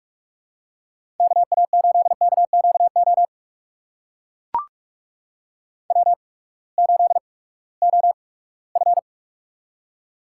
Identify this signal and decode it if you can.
Morse code, heard as a keyed tone, about 45 wpm, spaced as for 22 wpm, 700 Hz. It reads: KA8KQO W8OF
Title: USA Call Signs - 8x Character Spacing - 45wpm